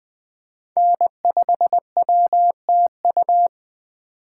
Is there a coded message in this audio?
Yes: Morse code from 0.8 to 3.5 s, heard as a keyed tone, about 20 wpm, 700 Hz. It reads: N5WTU